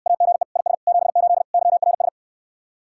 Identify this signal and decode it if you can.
Morse code, heard as a keyed tone, about 69 wpm, 700 Hz. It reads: AGE IS 78 YRS